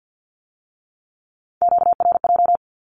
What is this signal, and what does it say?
Morse code, heard as a keyed tone, about 50 wpm, 700 Hz. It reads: QRJ
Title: Q-Codes with Spoken Meaning - 50wpm